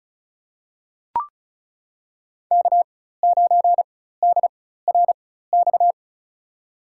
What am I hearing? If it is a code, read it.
Morse code, heard as a keyed tone, about 36 wpm, spaced as for 16 wpm, 700 Hz. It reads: K9DRX
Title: USA Call Signs - 5 Characters - 4x Character Spacing - 35wpm